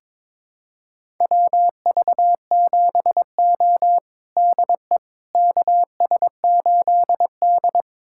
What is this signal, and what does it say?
Morse code, heard as a keyed tone, about 22 wpm, 700 Hz. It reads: WV7O DE KS8D
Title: USA Callsign DE Callsign - 4 Characters - 22wpm